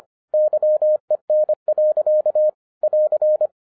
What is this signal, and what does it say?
Morse code, heard as a keyed tone, about 25 wpm, 610 Hz. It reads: YEN. <AR>